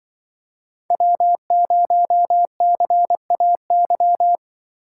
Morse code, heard as a keyed tone, about 25 wpm, 700 Hz. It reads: W0CAY
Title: Speed Racing - USA Call Signs 20wpm